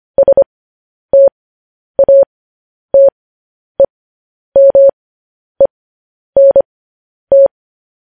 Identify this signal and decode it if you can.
Morse code, heard as a keyed tone, about 25 wpm, spaced as for 10 wpm, 550 Hz. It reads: STATEMENT
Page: news-headlines-in-morse-code-at-10-wpm